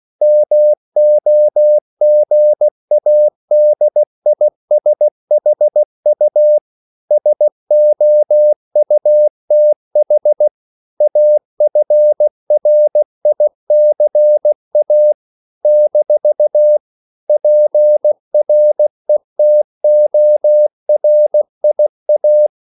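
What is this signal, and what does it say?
Morse code, heard as a keyed tone, about 16 wpm, 600 Hz. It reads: MOGADISHU SOUTH AFRICA - PRETORIA
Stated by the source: CW practice of national capitals at 16 words per minute, from ON6ZQ.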